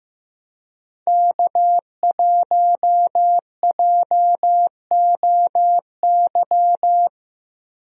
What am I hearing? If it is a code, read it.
Morse code, heard as a keyed tone, about 15 wpm, 700 Hz. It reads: K1JOY
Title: USA Call Signs - 5 Characters - Repeated 2x - 15wpm